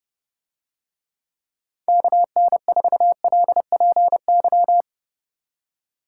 Morse code, heard as a keyed tone, about 29 wpm, 700 Hz. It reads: KN4LPY